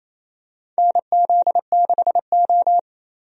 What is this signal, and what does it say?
Morse code, heard as a keyed tone, about 28 wpm, 700 Hz. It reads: NZ6O
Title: USA Call Signs - 4 Characters - Repeated 3x - 28wpm